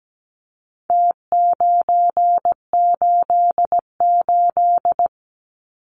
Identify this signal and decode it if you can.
Morse code, heard as a keyed tone, about 17 wpm, 700 Hz. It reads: T988
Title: Numbers - Contest Style - 4-Digit 17wpm